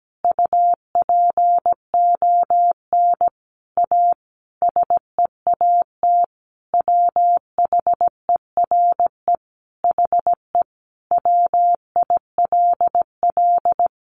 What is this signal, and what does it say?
Morse code, heard as a keyed tone, about 17 wpm, 700 Hz. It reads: UPON A SEAT WHERE HE WILL
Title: Sentences from Top 1500 Words 17wpm